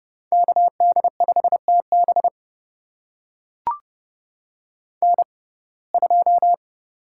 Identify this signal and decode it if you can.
Morse code, heard as a keyed tone, about 32 wpm, spaced as for 22 wpm, 690 Hz. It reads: KD5TB N2
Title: USA Call Signs - 5 Characters - 6x Character Spacing - 30wpm